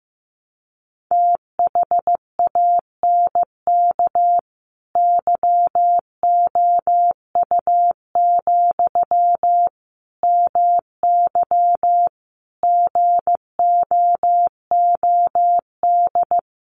Morse code, heard as a keyed tone, about 14 wpm, 700 Hz. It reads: THANK YOU, MY GOOD
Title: Sentences from Top 600 Words 15wpm